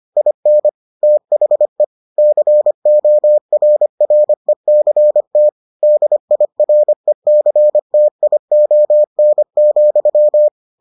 Morse code, heard as a keyed tone, about 25 wpm, 600 Hz. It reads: IN THE CORRECT DIRECTION,